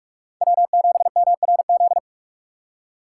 Morse code, heard as a keyed tone, about 43 wpm, 700 Hz. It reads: W7MRZ